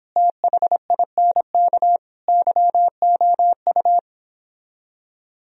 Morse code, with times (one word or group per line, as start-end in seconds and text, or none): 0.2-2.0 THINK
2.3-4.0 YOU